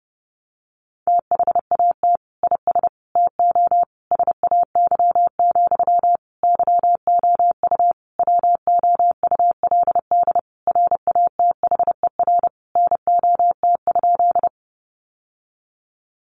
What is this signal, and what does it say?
Morse code, heard as a keyed tone, about 30 wpm, 700 Hz. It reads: THAT IS TO SAY, YOU WOULD RATHER NOT?
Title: Sentences from Top 700 Words 30wpm